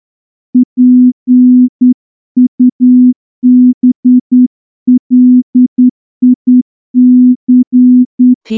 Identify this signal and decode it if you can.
Morse code, heard as a keyed tone, about 10 wpm, 250 Hz. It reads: PUBLIC
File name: morse-training-10WPM